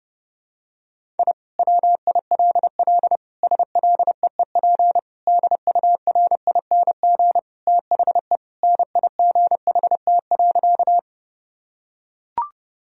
Morse code, heard as a keyed tone, about 32 wpm, 700 Hz. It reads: I WILL SLEEP DURING THE NIGHT.
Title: Sentences from Top 1000 Words - Review Part 2 - 30wpm